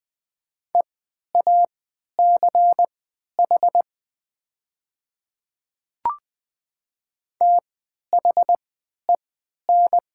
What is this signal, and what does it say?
Morse code, heard as a keyed tone, about 20 wpm, spaced as for 11 wpm, 700 Hz. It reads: EACH THEN
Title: 3x Character Spacing - Top 100 Words - 20wpm